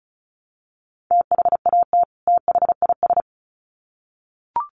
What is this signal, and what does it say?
Morse code, heard as a keyed tone, about 34 wpm, 700 Hz. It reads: THAT THIS
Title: Sets of 2 Words 35wpm